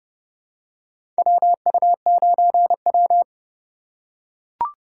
Morse code, heard as a keyed tone, about 31 wpm, 700 Hz. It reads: WU9W
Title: USA Call Signs - 4 Characters - 2x Character Spacing - 30wpm